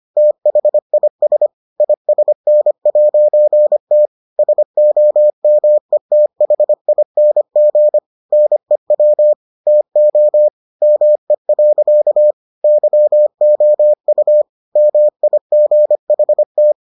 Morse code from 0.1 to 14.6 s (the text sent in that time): THIS ISN'T SOMETHING NEW TO ME. YOU